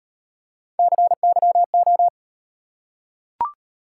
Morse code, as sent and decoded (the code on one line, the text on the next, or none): -.-. -.-- ---
CYO